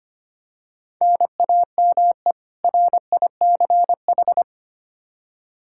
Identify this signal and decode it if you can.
Morse code, heard as a keyed tone, about 25 wpm, 700 Hz. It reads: NAME RICH
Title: K1USN SST QSO Practice 25wpm